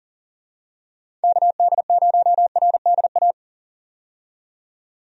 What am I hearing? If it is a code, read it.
Morse code, heard as a keyed tone, about 40 wpm, 700 Hz. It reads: KD0RDA
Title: USA Call Signs - 6 Characters - 40wpm